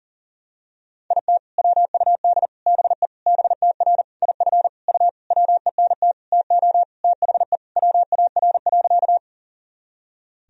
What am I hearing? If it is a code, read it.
Morse code, heard as a keyed tone, about 40 wpm, 700 Hz. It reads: IT WUD BE BTR IF U WENT TO THE WAR.
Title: Sentences from Top 400 Words and Abbreviations 40wpm